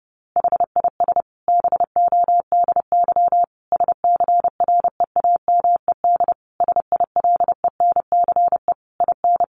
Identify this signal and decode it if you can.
Morse code, heard as a keyed tone, about 30 wpm, 700 Hz. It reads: HIS BODY SCREAMED SILENCE IN
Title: A Sound Of Thunder 30wpm